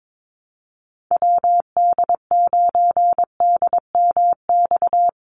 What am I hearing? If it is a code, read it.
Morse code, heard as a keyed tone, about 22 wpm, 700 Hz. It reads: WD9DMX
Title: USA Call Signs 22wpm